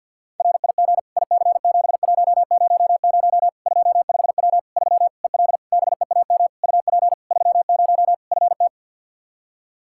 Morse code, heard as a keyed tone, about 52 wpm, 700 Hz. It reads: RIG IC7100 25W 3 EL BEAM UP 30 FT
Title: Practice QSO - Exchange 2 - Info - 50wpm